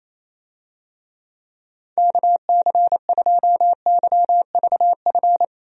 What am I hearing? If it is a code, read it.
Morse code, heard as a keyed tone, about 27 wpm, 700 Hz. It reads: KC2YVF
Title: USA Call Signs - 6 Characters - 28wpm